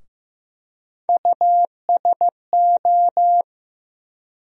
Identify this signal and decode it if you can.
Morse code, heard as a keyed tone, about 15 wpm, 700 Hz. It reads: USO
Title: USA Call Sign Postfixes - 3 Letters 15wpm